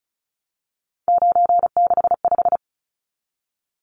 Morse code, heard as a keyed tone, about 34 wpm, 700 Hz. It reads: 965
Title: Numbers - 3-Digit 35wpm